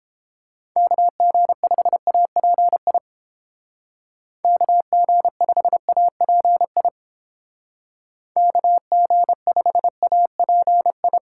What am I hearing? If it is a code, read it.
Morse code, heard as a keyed tone, about 30 wpm, 700 Hz. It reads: KG5API KG5API KG5API